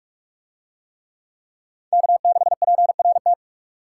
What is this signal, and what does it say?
Morse code, heard as a keyed tone, about 44 wpm, 700 Hz. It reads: K6PRT